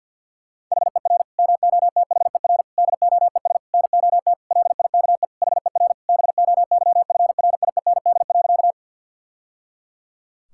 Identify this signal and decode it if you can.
Morse code, heard as a keyed tone, about 50 wpm, 700 Hz. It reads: HER MOTHER DOES NOT LIKE HER BOYFRIEND.
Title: Sentences from Top 1000 Words - Review Part 5 - 50wpm